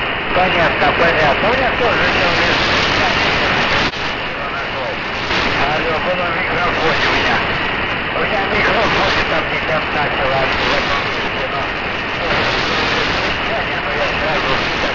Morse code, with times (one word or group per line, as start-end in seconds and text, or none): none